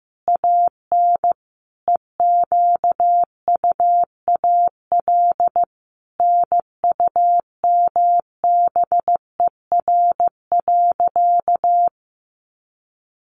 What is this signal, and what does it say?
Morse code, heard as a keyed tone, about 15 wpm, 700 Hz. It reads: AN EQUAL NUMBER.
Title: Sentences from Top 1400 Words 15wpm